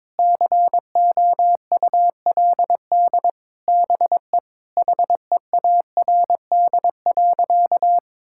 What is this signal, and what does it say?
Morse code, heard as a keyed tone, about 22 wpm, 700 Hz. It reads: COULD BE HEARD.
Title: Sentences from Top 1000 Words - Review Part 5 - 22wpm